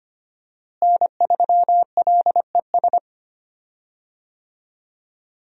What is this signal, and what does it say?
Morse code, heard as a keyed tone, about 25 wpm, 700 Hz. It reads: N3LES
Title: USA Call Signs - 5 Characters - Repeated 2x - 25wpm